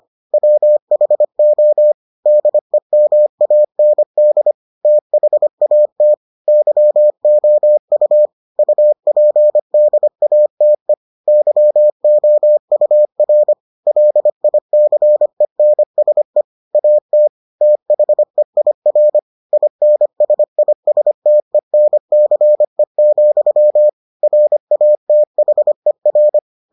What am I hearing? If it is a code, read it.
Morse code, heard as a keyed tone, about 25 wpm, 600 Hz. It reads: WHO DEMAND THAT YOU UPDATE YOUR LICENSE AT THEIR INSISTENCE, RATHER